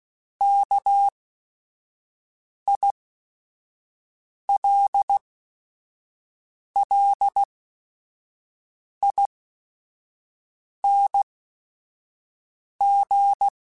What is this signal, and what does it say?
Morse code, heard as a keyed tone, about 16 wpm, spaced as for 5 wpm, 800 Hz. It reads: KILLING
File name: Special INGs / Special INGs-mp3s / Sngl Ing @5